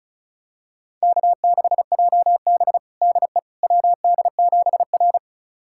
Morse code, heard as a keyed tone, about 36 wpm, 700 Hz. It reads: K6JB DE WD7R
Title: USA Callsign DE Callsign - 4 Characters - 35wpm